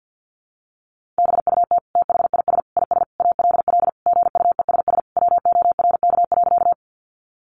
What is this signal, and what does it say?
Morse code, heard as a keyed tone, about 48 wpm, 700 Hz. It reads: BUT THIS IS ALL GUESS WORK.
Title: Sentences from Top 1500 Words 50wpm